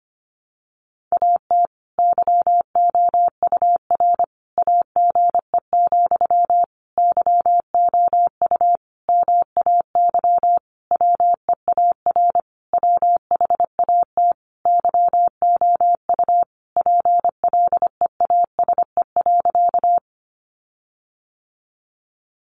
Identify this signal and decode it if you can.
Morse code, heard as a keyed tone, about 25 wpm, 700 Hz. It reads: AT YOUR AGE, YOU MAY WEAR WHAT YOU PLEASE.
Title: Sentences from Top 1500 Words 25wpm